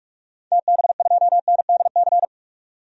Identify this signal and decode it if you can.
Morse code, heard as a keyed tone, about 46 wpm, 700 Hz. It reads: TB2NDC
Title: DXCC-100 Call Signs - 6 Characters - 45wpm